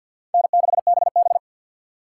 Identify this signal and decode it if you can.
Morse code, heard as a keyed tone, about 64 wpm, 700 Hz. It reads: N6BB